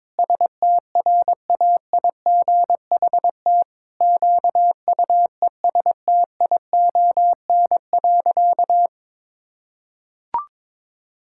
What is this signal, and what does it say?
Morse code, heard as a keyed tone, about 22 wpm, 700 Hz. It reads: STRAIGHT QUESTION.
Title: Sentences from Top 1000 Words - Review Part 6 - 22wpm